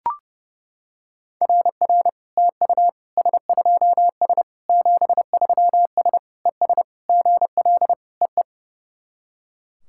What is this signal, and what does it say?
Morse code, heard as a keyed tone, about 28 wpm, 690 Hz. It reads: RR TU S2S 73S ES GL EE
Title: SOTA QSO Practice 30wpm